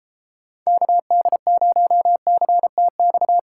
Morse code, heard as a keyed tone, about 31 wpm, 700 Hz. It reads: KD0CTX